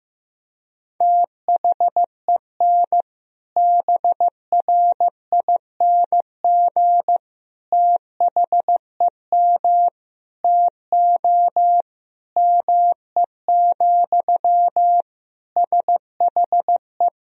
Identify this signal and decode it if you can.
Morse code, heard as a keyed tone, about 15 wpm, 700 Hz. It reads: THEN BRING THEM TO ME, SHE
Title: Sentences from Top 500 Words and Abbreviations 15wpm